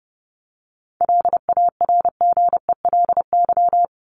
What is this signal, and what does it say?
Morse code, heard as a keyed tone, about 30 wpm, 700 Hz. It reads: LARGELY
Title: LY Suffix - Top 100 Words with Base Words 30wpm